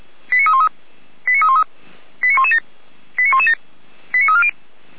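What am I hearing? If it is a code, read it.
Five-tone selective call (ZVEI), heard as quick runs of five tones, tones of about 70 ms. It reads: 89313 89313 89117 89117 89340
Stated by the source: ZVEI five-tone selective call received by SDR, logged by Feuerwehr Orlamünde fire brigade.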